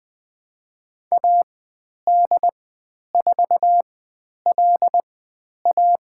Morse code, heard as a keyed tone, about 20 wpm, spaced as for 10 wpm, 700 Hz. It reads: AD4LA
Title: 095 Lesson 16g - Letter c - Call Signs 20wpm